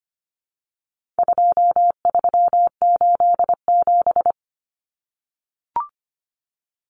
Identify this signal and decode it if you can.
Morse code, heard as a keyed tone, about 25 wpm, 700 Hz. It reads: 2387
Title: Common Cut Numbers - 4-Digit 25wpm